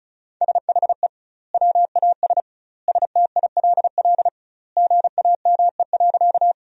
Morse code, heard as a keyed tone, about 34 wpm, 700 Hz. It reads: SHE WAS STILL GAME.
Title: 2x Word-Spacing - Sentences from Top 500 Words - 35wpm